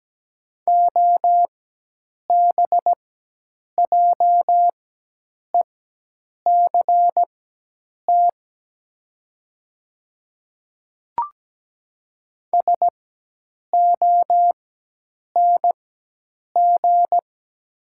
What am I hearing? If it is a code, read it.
Morse code, heard as a keyed tone, about 17 wpm, spaced as for 8 wpm, 700 Hz. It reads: OBJECT SONG
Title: 4x Character Spacing - Top 500 Words - 17wpm